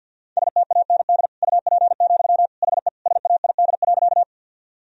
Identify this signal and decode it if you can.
Morse code, heard as a keyed tone, about 51 wpm, 700 Hz. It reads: STAND UP, HE SAID.
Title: Sentences from Top 1000 Words - Review Part 1 - 50wpm